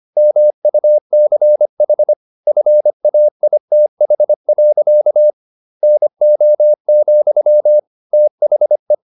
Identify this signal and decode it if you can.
Morse code, heard as a keyed tone, about 25 wpm, 600 Hz. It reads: MUCH FAITH. NO, THE
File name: the old man and the sea part 1 25wpm